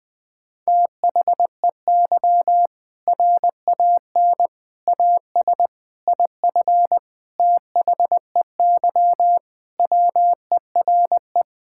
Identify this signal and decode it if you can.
Morse code, heard as a keyed tone, about 20 wpm, 700 Hz. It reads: THEY RAN AS IF THEY WERE
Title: Sentences with 4-Word Phrases - 20wpm